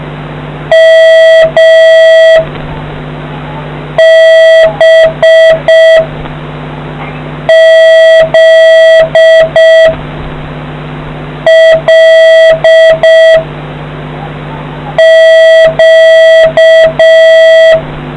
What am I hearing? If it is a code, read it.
Morse code, heard as a keyed tone, about 6 wpm, 650 Hz, with noise. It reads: M B Z L Q